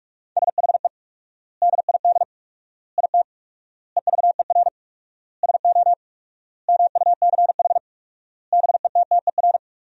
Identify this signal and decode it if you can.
Morse code, heard as a keyed tone, about 46 wpm, 700 Hz. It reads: SHE DID IT EVER SO MUCH BETTER
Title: 4x Word-Spacing - Sentences from Top 400 Words - 45wpm